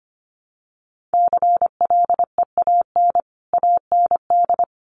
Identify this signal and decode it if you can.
Morse code, heard as a keyed tone, about 25 wpm, 700 Hz. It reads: CLEAN AND
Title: Sentences from Top 1100 Words 25wpm